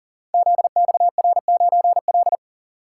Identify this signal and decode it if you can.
Morse code, heard as a keyed tone, about 43 wpm, 700 Hz. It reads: ZXR9L
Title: Code Groups - 5 Letter:Numbers - 40wpm